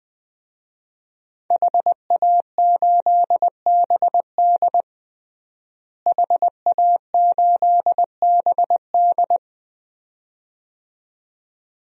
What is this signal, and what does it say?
Morse code, heard as a keyed tone, about 21 wpm, 700 Hz. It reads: HA8BD HA8BD
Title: DXCC-100 Call Signs - 5 Characters - Repeated 2x - 20wpm